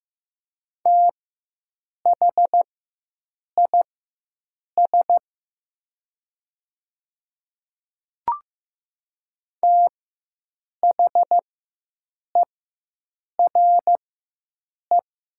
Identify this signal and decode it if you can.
Morse code, heard as a keyed tone, about 15 wpm, spaced as for 7 wpm, 700 Hz. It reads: THIS THERE
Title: 4x Character Spacing - Top 200 Words - 15wpm